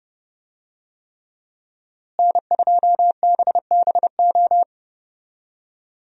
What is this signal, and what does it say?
Morse code, heard as a keyed tone, about 30 wpm, 700 Hz. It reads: N2BBO
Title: USA Call Signs - 5 Characters - Repeated 3x - 30wpm